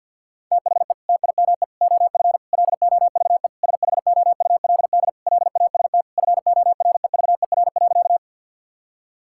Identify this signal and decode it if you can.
Morse code, heard as a keyed tone, about 50 wpm, 700 Hz. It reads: THE TIME OF LOVE SHOULD LAST FOREVER.
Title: Sentences from Top 1600 Words 50wpm